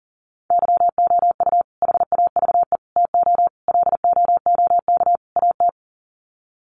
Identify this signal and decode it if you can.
Morse code, heard as a keyed tone, about 40 wpm, 700 Hz. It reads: YOU HAVE TO LOOK AT
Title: Sets of 5 Words Encore 40wpm